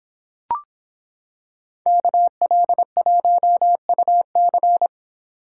Morse code, heard as a keyed tone, about 26 wpm, 700 Hz. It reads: KL1UC